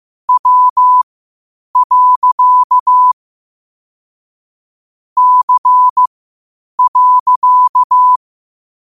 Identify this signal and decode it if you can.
Morse code, heard as a keyed tone, about 15 wpm, spaced as for 8 wpm, 1000 Hz. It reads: W. C.
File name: qotdfiles / 8wpm